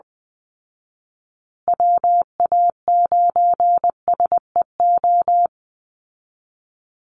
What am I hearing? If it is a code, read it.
Morse code, heard as a keyed tone, about 20 wpm, 700 Hz. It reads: WA9SEO
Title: Call Signs Repeated 2x - 20wpm